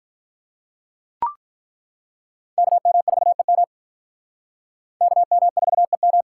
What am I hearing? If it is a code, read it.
Morse code, heard as a keyed tone, about 42 wpm, 700 Hz. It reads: KM4EM KM4EM
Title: Speed Racing - USA 2x2 Call Signs 35wpm